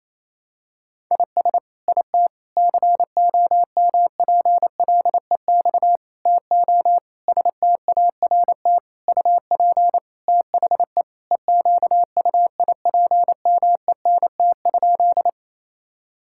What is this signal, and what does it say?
Morse code, heard as a keyed tone, about 28 wpm, 700 Hz. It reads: IS IT COMPLEX TO START UP THE EQUIPMENT?
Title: Sentences from Top 1000 Words - Review Part 6 - 28wpm